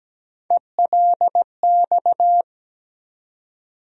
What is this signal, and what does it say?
Morse code, heard as a keyed tone, about 17 wpm, 700 Hz. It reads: ELX
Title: USA Call Sign Postfixes - 3 Letters 17wpm